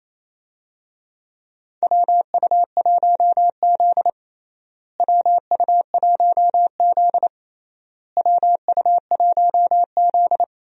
Morse code, heard as a keyed tone, about 28 wpm, 700 Hz. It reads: WU1Z WU1Z WU1Z
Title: USA Call Signs - 4 Characters - Repeated 3x - 28wpm